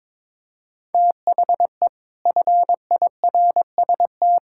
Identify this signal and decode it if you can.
Morse code, heard as a keyed tone, about 22 wpm, 700 Hz. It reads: THE FIRST
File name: practice / 22wpm / Sets of 8 Words - 22wpm